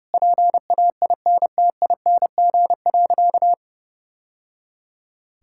Morse code, heard as a keyed tone, about 28 wpm, 700 Hz. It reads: PAINTING.
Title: Sentences from Top 1000 Words - Review Part 6 - 30wpm